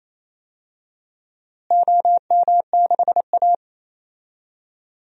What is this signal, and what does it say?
Morse code, heard as a keyed tone, about 28 wpm, 700 Hz. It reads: OM6A
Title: DXCC-100 Call Signs - 4 Characters - Repeated 3x - 28wpm